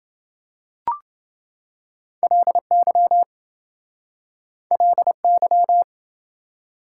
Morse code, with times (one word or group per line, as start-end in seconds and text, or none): 2.2-3.2 LY
4.7-5.8 LY